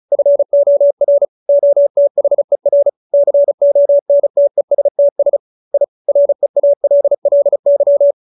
Morse code, heard as a keyed tone, about 34 wpm, 570 Hz. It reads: FOR OTHER CONTESTS I REALLY